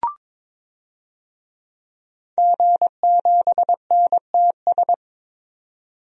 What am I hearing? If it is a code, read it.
Morse code, heard as a keyed tone, about 22 wpm, 700 Hz. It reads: G7NTS
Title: DXCC-100 Call Signs - 5 Characters - 22wpm